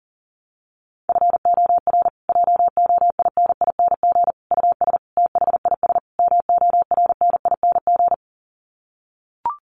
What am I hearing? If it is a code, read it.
Morse code, heard as a keyed tone, about 37 wpm, 700 Hz. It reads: FOR JOINING US THIS MORNING
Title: Sets of 5 Words Encore 40wpm